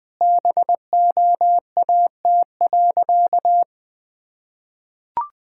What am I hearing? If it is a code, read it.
Morse code, heard as a keyed tone, about 21 wpm, 700 Hz. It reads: BOAT.
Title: Sentences with 3-Word Phrases - 20wpm